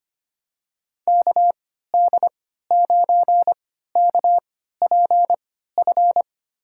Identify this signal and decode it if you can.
Morse code, heard as a keyed tone, about 25 wpm, spaced as for 14 wpm, 700 Hz. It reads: KD9KPF
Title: USA Call Signs - 6 Characters - 3x Character Spacing - 25wpm